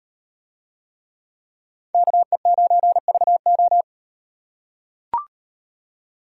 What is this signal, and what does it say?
Morse code, heard as a keyed tone, about 38 wpm, 700 Hz. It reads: KE9VO